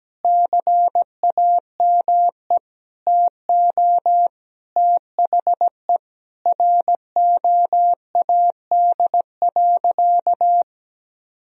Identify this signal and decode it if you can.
Morse code, heard as a keyed tone, about 17 wpm, 700 Hz. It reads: CAME TO THE ROAD.
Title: Sentences from Top 1000 Words - Review Part 2 - 17wpm